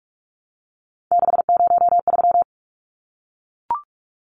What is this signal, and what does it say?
Morse code, heard as a keyed tone, about 44 wpm, 700 Hz. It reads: 603
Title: Numbers - 3-Digit 45wpm